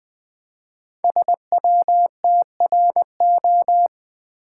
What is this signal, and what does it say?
Morse code, heard as a keyed tone, about 20 wpm, 700 Hz. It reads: SWTRO
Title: Code Groups - 5 Letters - 20wpm